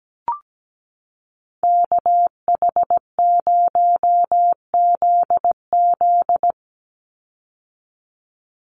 Morse code, heard as a keyed tone, about 17 wpm, 700 Hz. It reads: KH0ZZ
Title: Call Signs - 17wpm